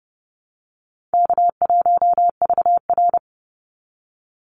Morse code, heard as a keyed tone, about 30 wpm, 700 Hz. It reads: K1VR
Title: Call Signs Repeated 2x - 30wpm